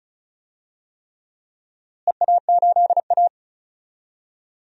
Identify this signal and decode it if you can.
Morse code, heard as a keyed tone, about 36 wpm, 700 Hz. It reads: EA8A
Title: DXCC-100 Call Signs - 4 Characters - Repeated 3x - 35wpm